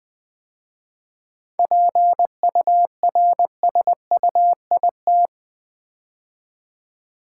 Morse code, heard as a keyed tone, about 20 wpm, 700 Hz. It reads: PURSUIT
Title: 7 Letter Words - First 3 Letters Given - 20wpm